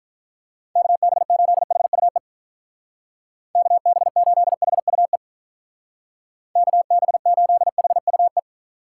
Morse code, heard as a keyed tone, about 44 wpm, 700 Hz. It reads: KB8HUE KB8HUE KB8HUE